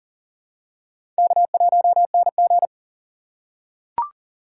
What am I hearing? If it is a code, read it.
Morse code, heard as a keyed tone, about 37 wpm, 700 Hz. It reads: K1NG
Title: USA Call Signs - 4x Character Spacing - 40wpm